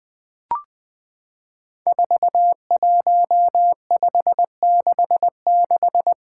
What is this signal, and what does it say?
Morse code, heard as a keyed tone, about 20 wpm, 700 Hz. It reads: 41566
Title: Five Challenging Letter-Number Combinations 20wpm